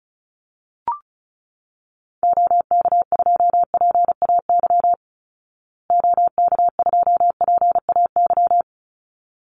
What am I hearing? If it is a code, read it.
Morse code, heard as a keyed tone, about 34 wpm, 700 Hz. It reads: OK2PAY OK2PAY
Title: Call Signs Repeated 3x - 35wpm